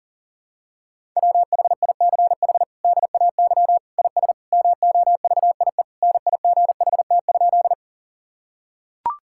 Morse code, heard as a keyed tone, about 40 wpm, 700 Hz. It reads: WHICH DAY IS MOVIE NIGHT?
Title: Sentences from Top 1000 Words - Review Part 5 - 40wpm